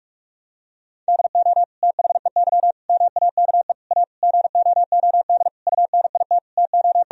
Morse code, heard as a keyed tone, about 46 wpm, 700 Hz. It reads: DO THEY MAKE A GOOD UNIT TO
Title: Sentences from Top 1000 Words - Review Part 2 - 45wpm